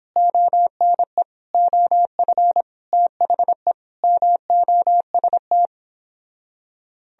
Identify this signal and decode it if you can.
Morse code, heard as a keyed tone, about 31 wpm, 700 Hz. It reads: ONE OF THE MOST